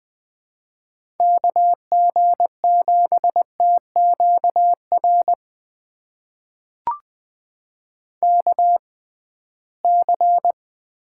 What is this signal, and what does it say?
Morse code, heard as a keyed tone, about 20 wpm, spaced as for 15 wpm, 700 Hz. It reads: KG7TQR KC
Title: USA Call Signs - 6 Characters - 6x Character Spacing - 20wpm